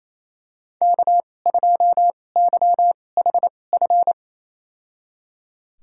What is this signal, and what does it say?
Morse code, heard as a keyed tone, about 27 wpm, spaced as for 20 wpm, 700 Hz. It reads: K2YHF
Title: USA Call Signs - 5 Characters - 2x Character Spacing - 28wpm